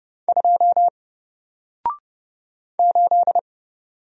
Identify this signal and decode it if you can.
Morse code, heard as a keyed tone, about 30 wpm, 700 Hz. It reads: 2 8
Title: Single Character (Letter:Number:Punctuation:Prosign) 30wpm